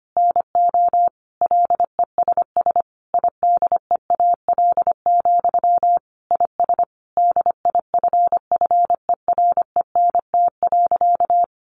Morse code, heard as a keyed tone, about 25 wpm, 700 Hz. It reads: NO LESS IDEAL, IS DIFFERENT.
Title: Sentences from Top 1400 Words 25wpm